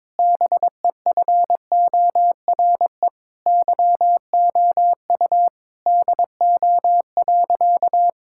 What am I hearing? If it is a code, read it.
Morse code, heard as a keyed tone, about 22 wpm, 700 Hz. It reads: BEFORE YOU DO.
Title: Sentences from Top 1000 Words - Review Part 6 - 22wpm